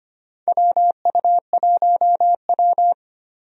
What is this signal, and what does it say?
Morse code, heard as a keyed tone, about 25 wpm, 700 Hz. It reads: WU1W
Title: USA Call Signs - 4 Characters - 25wpm